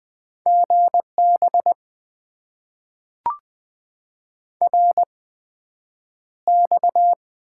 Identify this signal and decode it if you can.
Morse code, heard as a keyed tone, about 20 wpm, spaced as for 9 wpm, 700 Hz. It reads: GB RX
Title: Top 90 QSO Elements - 8x Character Spacing - 20wpm